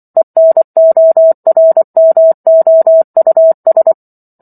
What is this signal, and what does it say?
Morse code, heard as a keyed tone, about 24 wpm, 650 Hz. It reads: ENORMOUS
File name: Chapter 15 - 24 WPM